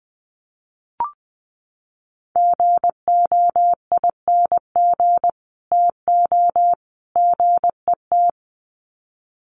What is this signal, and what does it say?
Morse code, heard as a keyed tone, about 20 wpm, 700 Hz. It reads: GOING TO GET
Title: Sets of 3 Words 20wpm